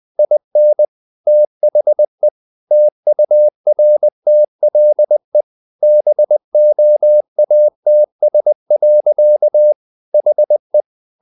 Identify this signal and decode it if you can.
Morse code, heard as a keyed tone, about 20 wpm, 600 Hz. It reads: IN THE TURTLE BOATS. HE
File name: the old man and the sea / the old man and the sea part 1 20wpm